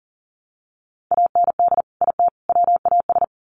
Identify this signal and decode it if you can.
Morse code, heard as a keyed tone, about 40 wpm, 700 Hz. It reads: AND IT WAS
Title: Sets of 3 Words 40wpm